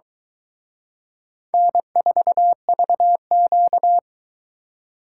Morse code, heard as a keyed tone, about 23 wpm, 700 Hz. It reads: N4VQ